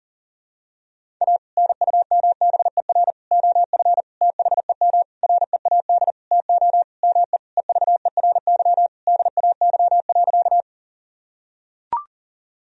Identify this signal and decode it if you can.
Morse code, heard as a keyed tone, about 40 wpm, 700 Hz. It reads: A NUMBER OF THEM READ TO ME EVERY DAY.
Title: Sentences from Top 1000 Words - Review Part 1 - 40wpm